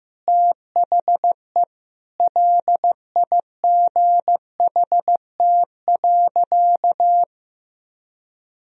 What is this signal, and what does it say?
Morse code, heard as a keyed tone, about 16 wpm, 700 Hz. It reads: THE LIGHT.
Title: Sentences from Top 1000 Words - Review Part 1 - 15wpm